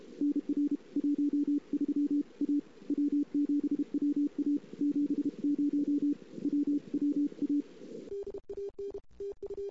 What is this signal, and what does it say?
Morse code, heard as a keyed tone, about 32 wpm, 290 Hz. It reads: NR 13 A W7WA 70 WWA
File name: CW / 2008-ss-cw-k5tr-op-n5rz -02:00